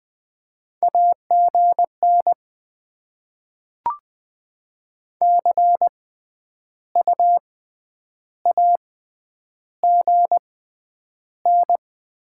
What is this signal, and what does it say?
Morse code, heard as a keyed tone, about 20 wpm, spaced as for 9 wpm, 700 Hz. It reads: AGN CU AGN
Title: Top 20 QSO Elements - 6x Character Spacing - 20wpm